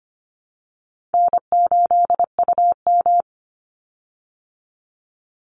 Morse code, heard as a keyed tone, about 25 wpm, 700 Hz. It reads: N8UM
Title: Easy Call Signs Repeated 3x - 25wpm